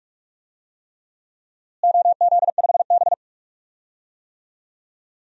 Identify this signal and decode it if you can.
Morse code, heard as a keyed tone, about 46 wpm, 700 Hz. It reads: OZ5B
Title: DXCC-100 Call Signs - 4 Characters - Repeated 3x - 45wpm